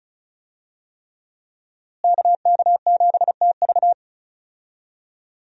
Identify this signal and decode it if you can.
Morse code, heard as a keyed tone, about 34 wpm, 700 Hz. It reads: KK7TV
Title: USA Call Signs - 5 Characters - Repeated 3x - 35wpm